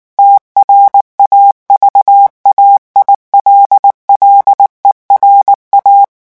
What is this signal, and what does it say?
Morse code, heard as a keyed tone, about 19 wpm, 800 Hz. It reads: TRAVAILLERA
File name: morse mp3 / revision9 citations humoristiques vitesse 18 mots minute